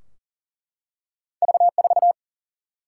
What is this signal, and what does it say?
Morse code, heard as a keyed tone, about 40 wpm, 700 Hz. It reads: V4
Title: Single-Letter Single-Number 40wpm